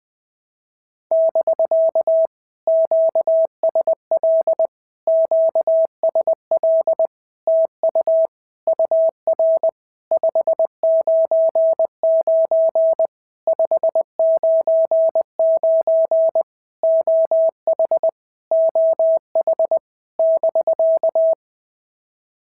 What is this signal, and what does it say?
Morse code, heard as a keyed tone, about 20 wpm, 650 Hz. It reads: <BK> QSL QSL TU UR 599 599 OH OH <BK>
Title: POTA QSO Practice - 20wpm